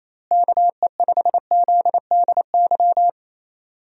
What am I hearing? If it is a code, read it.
Morse code, heard as a keyed tone, about 28 wpm, 700 Hz. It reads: KE5ZDY
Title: USA Call Signs - 6 Characters - Repeated 3x - 28wpm